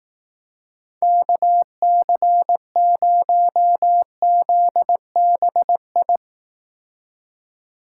Morse code, heard as a keyed tone, about 18 wpm, 700 Hz. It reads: KC0ZBI
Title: Speed Racing - USA 2x3 Call Signs 15wpm